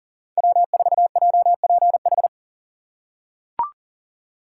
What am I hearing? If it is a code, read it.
Morse code, heard as a keyed tone, about 39 wpm, 700 Hz. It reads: W4JPH
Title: K1USN SST QSO Practice - 40wpm